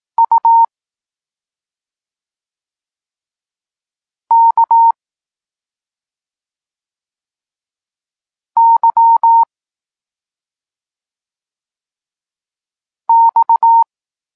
Morse code, heard as a keyed tone, about 18 wpm, 920 Hz. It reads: U K Y X
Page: Tekst 32 – Doskonalenie odbioru poznanych liter